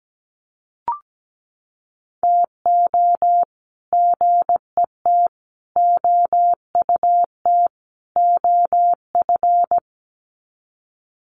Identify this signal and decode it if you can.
Morse code, heard as a keyed tone, about 17 wpm, 700 Hz. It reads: TO GET OUT OF
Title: Sets of 4 Words 17wpm